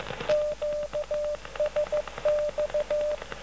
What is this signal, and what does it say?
Morse code, heard as a keyed tone, about 14 wpm, 600 Hz. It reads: QSX
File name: 4XO 13051.5 1445